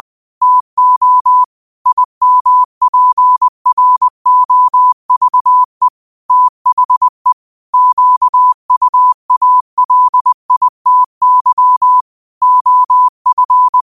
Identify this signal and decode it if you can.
Morse code, heard as a keyed tone, about 20 wpm, 1000 Hz. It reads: TO IMPROVE THE QUALITY OF